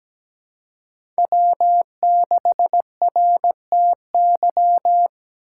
Morse code, heard as a keyed tone, about 17 wpm, 700 Hz. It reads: W6RTY
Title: USA Call Signs - 2x Character Spacing - 17wpm